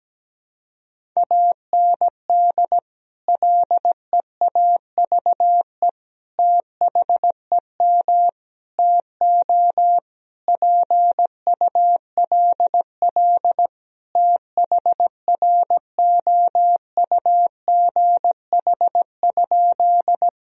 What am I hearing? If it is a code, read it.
Morse code, heard as a keyed tone, about 17 wpm, 700 Hz. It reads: AND LEAVE THEM TO PULL THROUGH?
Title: Sentences from Top 1000 Words - Review Part 2 - 17wpm